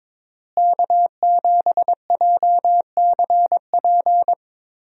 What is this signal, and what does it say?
Morse code, heard as a keyed tone, about 22 wpm, 700 Hz. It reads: K7JCP
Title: USA Call Signs - 5 Characters - Repeated 3x - 22wpm